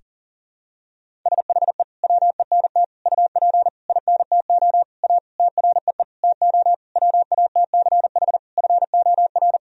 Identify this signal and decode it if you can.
Morse code, heard as a keyed tone, about 40 wpm, 700 Hz. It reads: SHE WENT UP INTO A TREE TO WATCH FOR
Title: Sentences from Top 1000 Words - Review Part 1 - 40wpm